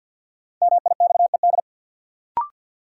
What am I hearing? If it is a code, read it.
Morse code, heard as a keyed tone, about 52 wpm, 700 Hz. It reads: MIXED
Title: Sentences from Top 1500 Words - New Word Drill - 50wpm